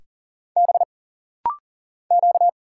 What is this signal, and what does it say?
Morse code, heard as a keyed tone, about 40 wpm, 700 Hz. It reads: B Q
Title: Single Letters - Rapid-Fire 40wpm